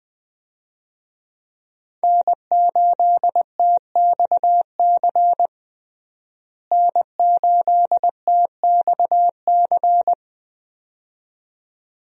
Morse code, heard as a keyed tone, about 21 wpm, 700 Hz. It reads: N8TXC N8TXC
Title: USA Call Signs - 5 Characters - Repeated 2x - 20wpm